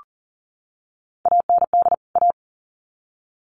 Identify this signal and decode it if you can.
Morse code, heard as a keyed tone, about 40 wpm, 700 Hz. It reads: AND A
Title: Sets of 2 Words 40wpm